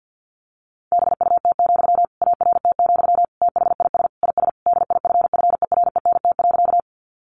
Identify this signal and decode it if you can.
Morse code, heard as a keyed tone, about 48 wpm, 700 Hz. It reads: BUT, ART, THIS IS DIFFERENT.
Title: Sentences from Top 600 Words 50wpm